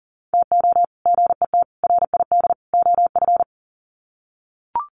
Morse code, heard as a keyed tone, about 40 wpm, 700 Hz. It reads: TO GET RID OF
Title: Sets of 4 Words 40wpm